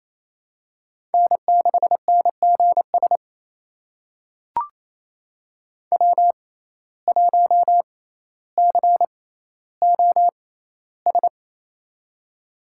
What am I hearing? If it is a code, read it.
Morse code, heard as a keyed tone, about 28 wpm, spaced as for 14 wpm, 700 Hz. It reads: N6NGS W1COS
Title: USA Call Signs - 5 Characters - 6x Character Spacing - 28wpm